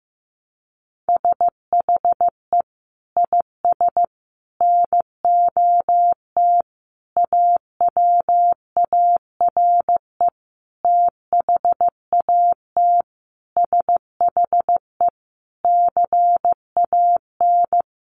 Morse code, heard as a keyed tone, about 15 wpm, 700 Hz. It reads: SHE IS NOT AWARE THAT SHE CAN
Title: Sentences from Top 1200 Words 15wpm